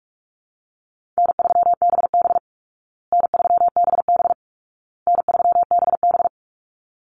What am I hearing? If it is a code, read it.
Morse code, heard as a keyed tone, about 44 wpm, 700 Hz. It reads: N3BB N3BB N3BB
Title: Easy Call Signs Repeated 3x - 45wpm